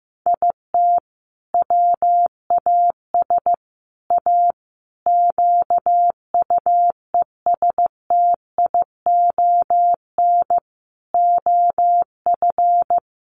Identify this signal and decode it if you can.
Morse code, heard as a keyed tone, about 14 wpm, 700 Hz. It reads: IT WAS A QUESTION OF
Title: Sentences from Top 1400 Words 15wpm